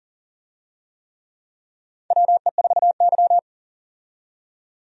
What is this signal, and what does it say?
Morse code, heard as a keyed tone, about 40 wpm, 700 Hz. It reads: WE4Y
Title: USA Call Signs - 4 Characters - 40wpm